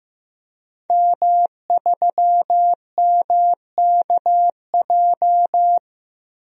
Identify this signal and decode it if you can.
Morse code, heard as a keyed tone, about 15 wpm, 700 Hz. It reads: M3MKJ
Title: DXCC-100 Call Signs - 5 Characters - Repeated 3x - 15wpm